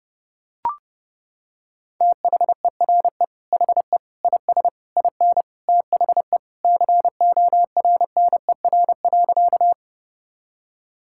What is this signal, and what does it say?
Morse code, heard as a keyed tone, about 30 wpm, 700 Hz. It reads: THERE HE IS IN THE CORNER.
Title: Sentences from Top 1600 Words 30wpm